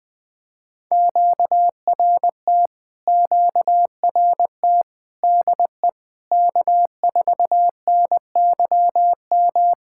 Morse code, heard as a keyed tone, about 20 wpm, 700 Hz. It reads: QRT QRT DE K4NYM
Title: POTA QSO Practice - 20wpm